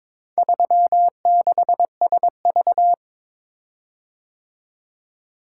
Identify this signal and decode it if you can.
Morse code, heard as a keyed tone, about 22 wpm, 700 Hz. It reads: 36SV
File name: practice / 22wpm / Four Letter-Number Combinations - 22wpm